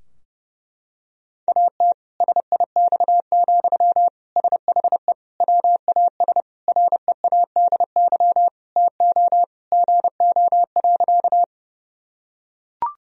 Morse code, heard as a keyed tone, about 32 wpm, 700 Hz. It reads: AT SIX, SHE WAS READY TO GO.
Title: Sentences from Top 1000 Words - Review Part 2 - 30wpm